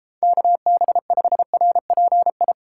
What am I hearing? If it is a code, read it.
Morse code, heard as a keyed tone, about 33 wpm, 700 Hz. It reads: KB5RPI